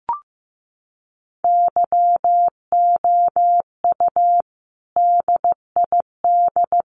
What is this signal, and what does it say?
Morse code, heard as a keyed tone, about 15 wpm, 700 Hz. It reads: YOU DID
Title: Sets of 3 Words 15wpm